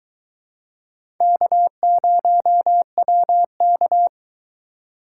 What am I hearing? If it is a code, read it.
Morse code, heard as a keyed tone, about 22 wpm, 700 Hz. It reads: K0WK